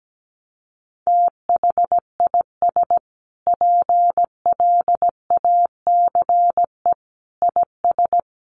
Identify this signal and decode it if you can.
Morse code, heard as a keyed tone, about 17 wpm, 700 Hz. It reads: THIS PLACE IS
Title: Sentences from Top 1200 Words 17wpm